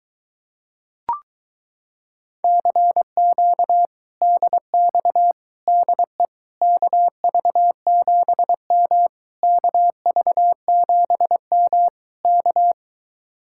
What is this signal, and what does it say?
Morse code, heard as a keyed tone, about 22 wpm, 700 Hz. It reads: CQ DX DE KV7M KV7M K